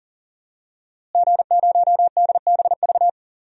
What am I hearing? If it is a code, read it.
Morse code, heard as a keyed tone, about 39 wpm, 700 Hz. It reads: G0DBV
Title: DXCC-100 Call Signs - 5 Characters - Repeated 2x - 40wpm